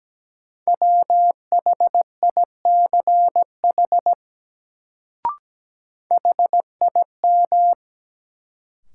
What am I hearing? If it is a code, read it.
Morse code, heard as a keyed tone, about 17 wpm, 700 Hz. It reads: WHICH HIM
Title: Sentences from Top 100 Words - New Word Drill - 17wpm